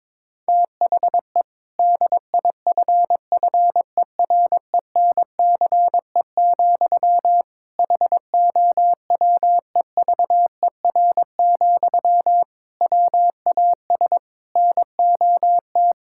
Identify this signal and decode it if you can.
Morse code, heard as a keyed tone, about 22 wpm, 700 Hz. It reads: THE DIFFERENCE, HOWEVER, WAS NOT
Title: Sentences from Top 1000 Words - Review Part 5 - 22wpm